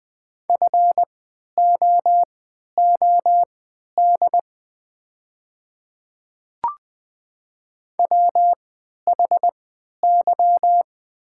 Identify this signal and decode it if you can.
Morse code, heard as a keyed tone, about 20 wpm, spaced as for 11 wpm, 690 Hz. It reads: FOOD WHY
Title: 3x Character Spacing - Top 500 Words - 20wpm